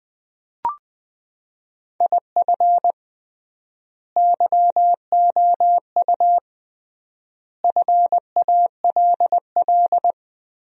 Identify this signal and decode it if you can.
Morse code, heard as a keyed tone, about 21 wpm, 700 Hz. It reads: IF YOU FALL
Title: 3x Word-Spacing - Sentences from Top 500 Words - 20wpm